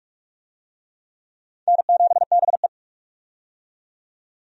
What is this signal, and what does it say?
Morse code, heard as a keyed tone, about 43 wpm, 700 Hz. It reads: N7BE
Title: USA Call Signs - 4 Characters - 45wpm